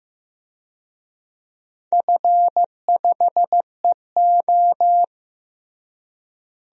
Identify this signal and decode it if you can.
Morse code, heard as a keyed tone, about 15 wpm, 700 Hz. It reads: F5EO
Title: DXCC-100 Call Signs - 4 Characters - Repeated 2x - 15wpm